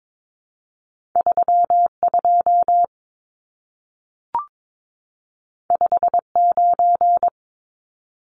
Morse code, heard as a keyed tone, about 22 wpm, 700 Hz. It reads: 32 59
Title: Numbers - 2-Digit 22wpm